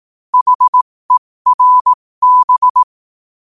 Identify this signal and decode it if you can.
Morse code, heard as a keyed tone, about 18 wpm, spaced as for 15 wpm, 1000 Hz. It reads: HERB
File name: qotdfiles / 15wpm